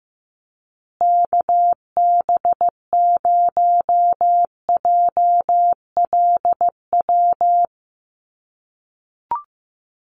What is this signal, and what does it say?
Morse code, heard as a keyed tone, about 15 wpm, 700 Hz. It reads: KB0JLW
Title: USA Call Signs Repeated 2x 15wpm